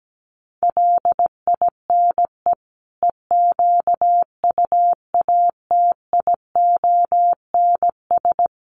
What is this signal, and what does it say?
Morse code, heard as a keyed tone, about 17 wpm, 700 Hz. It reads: LINE EQUATIONS
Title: Sets of 5 Words 17wpm